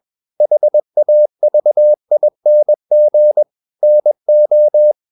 Morse code, heard as a keyed tone, about 21 wpm, 600 Hz. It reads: HAVING NO
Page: CW med Gnister